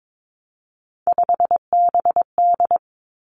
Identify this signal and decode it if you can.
Morse code, heard as a keyed tone, about 22 wpm, 700 Hz. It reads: 5BD
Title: Three Challenging Letter-Number Combinations 22wpm